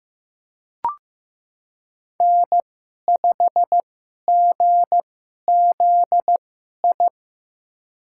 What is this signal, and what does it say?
Morse code, heard as a keyed tone, about 15 wpm, spaced as for 11 wpm, 700 Hz. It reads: N5GZI
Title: USA Call Signs - 5 Characters - 2x Character Spacing - 15wpm